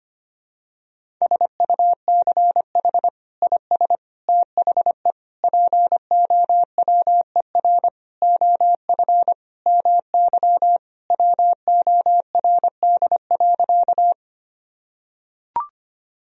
Morse code, heard as a keyed tone, about 25 wpm, 700 Hz. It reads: SUCH IS THE POWER OF MY WORD.
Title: Sentences from Top 1000 Words - Review Part 2 - 25wpm